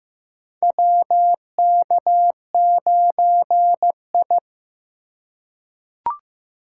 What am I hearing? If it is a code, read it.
Morse code, heard as a keyed tone, about 15 wpm, 700 Hz. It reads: WK9I
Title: USA Call Signs - 4 Characters - 4x Character Spacing - 15wpm